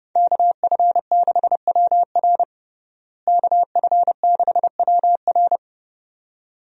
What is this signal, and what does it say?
Morse code, heard as a keyed tone, about 30 wpm, 700 Hz. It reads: KF6WR KF6WR
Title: USA Call Signs - 5 Characters - Repeated 2x - 30wpm